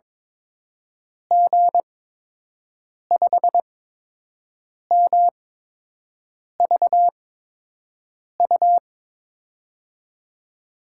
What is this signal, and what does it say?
Morse code, heard as a keyed tone, about 23 wpm, spaced as for 6 wpm, 700 Hz. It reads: G5MVU
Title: USA Call Signs - 6 Characters - 8x Character Spacing - 22wpm